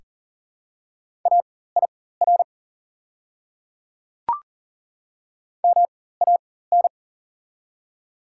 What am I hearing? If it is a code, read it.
Morse code, heard as a keyed tone, about 40 wpm, spaced as for 19 wpm, 700 Hz. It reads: AIR MAN
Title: 4x Character Spacing - Top 200 Words - 40wpm